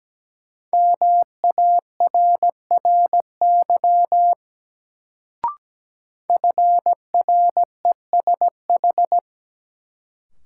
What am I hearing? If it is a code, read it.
Morse code, heard as a keyed tone, about 17 wpm, 700 Hz. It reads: MARRY FRESH
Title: Sentences from Top 1200 Words - New Word Drill Alt - 17wpm